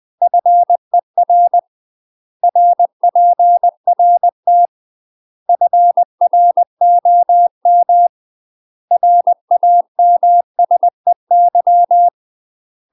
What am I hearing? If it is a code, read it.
Morse code, heard as a keyed tone, about 20 wpm, 700 Hz. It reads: FER RPRT FROM RAMSEY
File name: UK MASTER QSOs TRAINING / 20WPM / UK MASTER QSOs 0020